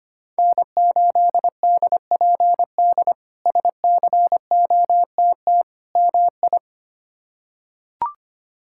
Morse code, heard as a keyed tone, about 25 wpm, 700 Hz. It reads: N8DPD SCOTT MI
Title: K1USN SST QSO Practice - 25wpm